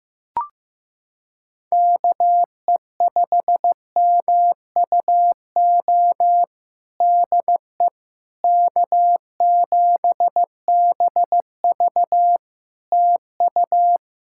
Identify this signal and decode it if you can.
Morse code, heard as a keyed tone, about 15 wpm, 700 Hz. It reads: KE5MUO DE K7BV TU
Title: K1USN SST QSO Practice 15wpm